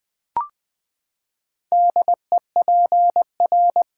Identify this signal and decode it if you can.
Morse code, heard as a keyed tone, about 20 wpm, 700 Hz. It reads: DEPR
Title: Build A Word - 4 Letters At A Time - 10-Letter Words - 20wpm